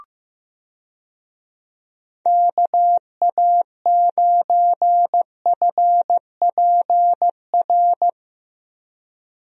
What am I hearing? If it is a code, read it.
Morse code, heard as a keyed tone, about 16 wpm, 700 Hz. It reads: KA9FPR
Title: USA Call Signs - 6 Characters - 15wpm